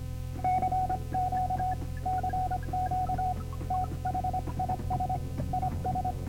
Morse code, heard as a keyed tone, about 25 wpm, 740 Hz, with noise. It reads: CQ CQ THIS IS